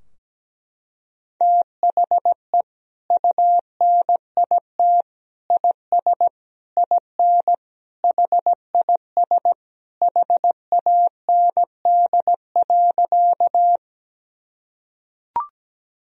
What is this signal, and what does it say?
Morse code, heard as a keyed tone, about 17 wpm, 700 Hz. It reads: THE UNIT IS IN HIS HAND.
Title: Sentences from Top 1000 Words - Review Part 2 - 17wpm